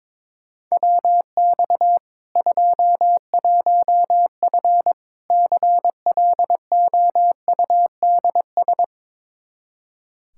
Morse code, heard as a keyed tone, about 21 wpm, 700 Hz. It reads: WX 21F CLOUDS
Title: K1USN SST QSO Practice 22wpm